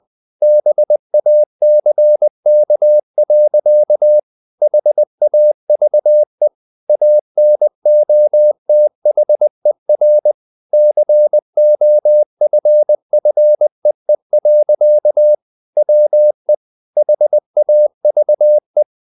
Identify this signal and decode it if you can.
Morse code, heard as a keyed tone, about 20 wpm, 600 Hz. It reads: BACK. HAVE ANOTHER COFFEE. WE HAVE
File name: the old man and the sea / the old man and the sea part 1 20wpm